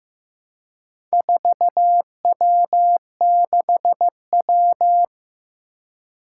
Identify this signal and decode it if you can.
Morse code, heard as a keyed tone, about 15 wpm, 700 Hz. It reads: 4W6W
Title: Four Challenging Letter-Number Combinations 15wpm